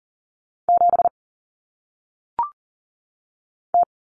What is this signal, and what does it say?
Morse code, heard as a keyed tone, about 40 wpm, 700 Hz. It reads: 7 T
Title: Common Cut Numbers - 1-Digit 40wpm